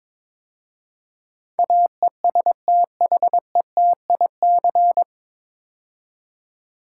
Morse code, heard as a keyed tone, about 22 wpm, 700 Hz. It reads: AESTHETIC
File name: 9 Letter Words - First 3 Letters Given - 22wpm